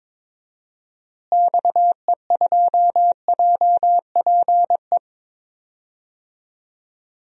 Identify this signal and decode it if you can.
Morse code, heard as a keyed tone, about 22 wpm, 700 Hz. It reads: XE2JPE
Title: DXCC-100 Call Signs - 6 Characters - Repeated 2x - 22wpm